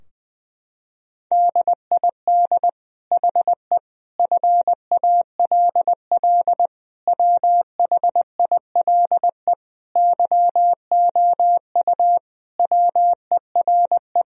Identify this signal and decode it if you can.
Morse code, heard as a keyed tone, about 21 wpm, 700 Hz. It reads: DID HE FALL WHILE YOU WERE
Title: Sentences from Top 1000 Words - Review Part 2 - 20wpm